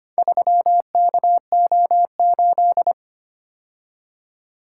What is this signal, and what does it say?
Morse code, heard as a keyed tone, about 25 wpm, 700 Hz. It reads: 3KO8
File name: practice / 25wpm / Four Letter-Number Combinations - 25wpm